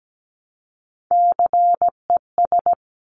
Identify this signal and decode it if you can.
Morse code, heard as a keyed tone, about 17 wpm, 700 Hz. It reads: CES
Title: Top Three Letter Combinations in Words 17wpm